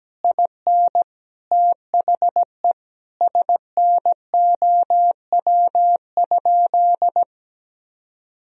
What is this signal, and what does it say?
Morse code, heard as a keyed tone, about 17 wpm, 700 Hz. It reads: IN THE SNOW?
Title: Sentences from Top 1000 Words - Review Part 3 - 17wpm